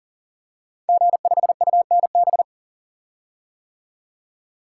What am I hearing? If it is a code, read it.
Morse code, heard as a keyed tone, about 40 wpm, 700 Hz. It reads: G5UNB
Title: Code Groups - 5 Letter:Numbers - 40wpm